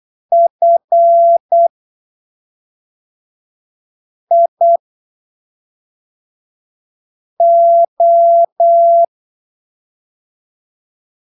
Morse code, heard as a keyed tone, about 8 wpm, spaced as for 3 wpm, 680 Hz. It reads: FIO
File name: arquivos / texto / texto 13 08ppm